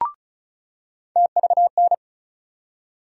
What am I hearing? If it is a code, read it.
Morse code, heard as a keyed tone, about 36 wpm, 700 Hz. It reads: TVN
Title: USA Call Sign Postfixes - 3 Letters 35wpm